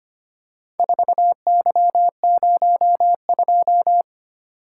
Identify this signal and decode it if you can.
Morse code, heard as a keyed tone, about 25 wpm, 700 Hz. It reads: 4Y02
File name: practice / 25wpm / Four Letter-Number Combinations - 25wpm